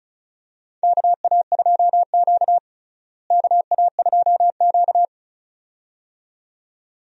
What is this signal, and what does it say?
Morse code, heard as a keyed tone, about 36 wpm, 700 Hz. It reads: KA2Q KA2Q
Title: USA Call Signs - 4 Characters - Repeated 2x - 35wpm